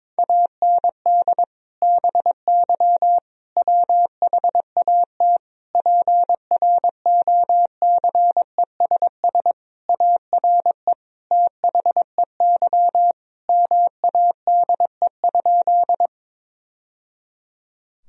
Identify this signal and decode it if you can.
Morse code, heard as a keyed tone, about 22 wpm, 700 Hz. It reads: AND BY WHAT PROCESS ARE THEY MADE?
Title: Sentences from Top 1000 Words - Review Part 3 - 22wpm